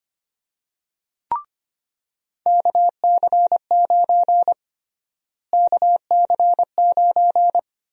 Morse code, heard as a keyed tone, about 25 wpm, 700 Hz. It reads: KC9 KC9
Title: USA Call Sign Prefixes - Repeated 2x - 25wpm